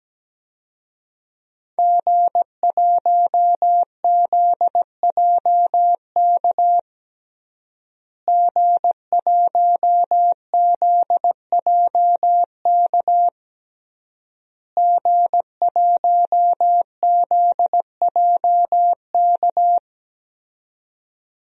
Morse code, heard as a keyed tone, about 17 wpm, 700 Hz. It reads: G1ZJK G1ZJK G1ZJK
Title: DXCC-100 Call Signs - 5 Characters - Repeated 3x - 17wpm